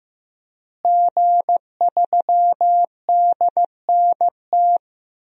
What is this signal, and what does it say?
Morse code, heard as a keyed tone, about 15 wpm, 700 Hz. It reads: G3DNT
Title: DXCC-100 Call Signs - 5 Characters - Repeated 3x - 15wpm